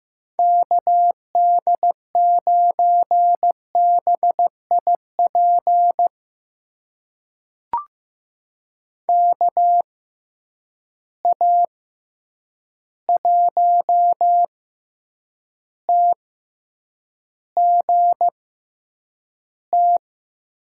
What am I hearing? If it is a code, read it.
Morse code, heard as a keyed tone, about 15 wpm, spaced as for 8 wpm, 700 Hz. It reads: KD9BIP KA1TGT
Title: USA Call Signs - 6 Characters - 6x Character Spacing - 15wpm